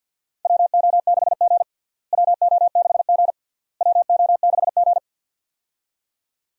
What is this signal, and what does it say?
Morse code, heard as a keyed tone, about 51 wpm, 700 Hz. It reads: WO6G WO6G WO6G
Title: USA Call Signs - 4 Characters - Repeated 3x - 50wpm